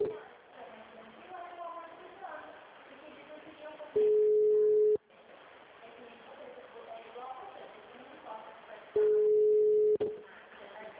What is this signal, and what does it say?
Ringback tone (Europe, 425 Hz), with speech.
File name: out- -2009- - - .2774